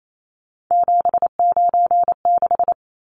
Morse code, heard as a keyed tone, about 28 wpm, 700 Hz. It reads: 796
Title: Numbers - 3-Digit 28wpm